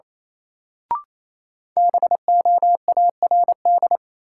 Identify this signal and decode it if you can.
Morse code, heard as a keyed tone, about 28 wpm, 700 Hz. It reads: BOARD
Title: Sentences from Top 700 Words - New Word Drill - 28wpm